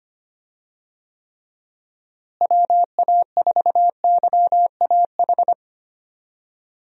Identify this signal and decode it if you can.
Morse code, heard as a keyed tone, about 25 wpm, 700 Hz. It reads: WA4YAH
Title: USA Call Signs - 6 Characters - Repeated 2x - 25wpm